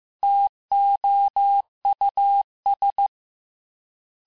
Morse code, heard as a keyed tone, about 15 wpm, 800 Hz. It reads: TOUS
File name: cours-du-lundi files / cours-1585-texte-FIN-de-QSO-vitesse-14mpm